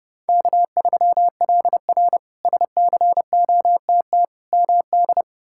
Morse code, heard as a keyed tone, about 29 wpm, 700 Hz. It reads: K3LR SCOTT MD
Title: K1USN SST QSO Practice - 30wpm